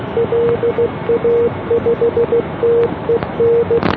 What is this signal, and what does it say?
Morse code, heard as a keyed tone, about 16 wpm, 470 Hz, with noise. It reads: LA5TEN